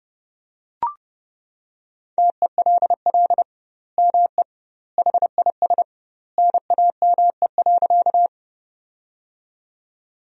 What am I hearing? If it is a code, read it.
Morse code, heard as a keyed tone, about 32 wpm, 700 Hz. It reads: TELL ME HIS NAME.
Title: 2x Word-Spacing - Sentences from Top 200 Words - 30wpm